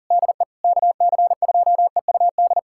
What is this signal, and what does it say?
Morse code, heard as a keyed tone, about 40 wpm, 700 Hz. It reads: DE KC2EUD
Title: USA Callsign DE Callsign 40wpm